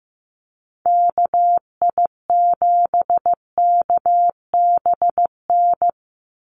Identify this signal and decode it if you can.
Morse code, heard as a keyed tone, about 15 wpm, 700 Hz. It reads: KI7KBN
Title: USA Call Signs 15wpm